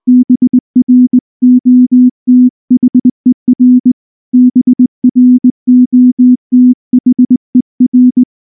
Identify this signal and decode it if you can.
Morse code, heard as a keyed tone, about 19 wpm, 250 Hz. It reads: BROTHER BROTHER
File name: morse / morse-training-20WPM-1